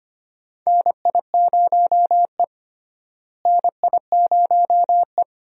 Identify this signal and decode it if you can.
Morse code, heard as a keyed tone, about 25 wpm, 700 Hz. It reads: NI0E NI0E
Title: USA Call Signs - 4 Characters - Repeated 2x - 25wpm